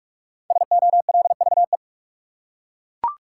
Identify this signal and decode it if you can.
Morse code, heard as a keyed tone, about 41 wpm, 700 Hz. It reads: SOLVE
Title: Five Letter Words - 45wpm